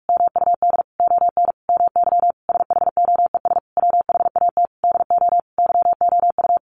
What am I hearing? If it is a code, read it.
Morse code, heard as a keyed tone, about 44 wpm, 700 Hz. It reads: MUD ON MY SHOES WHAT DO YOU
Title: A Sound Of Thunder 45wpm